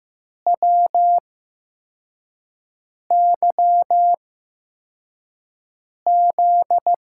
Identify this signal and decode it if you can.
Morse code, heard as a keyed tone, about 15 wpm, spaced as for 4 wpm, 700 Hz. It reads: WYZ
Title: USA Call Signs - 6 Characters - 8x Character Spacing - 15wpm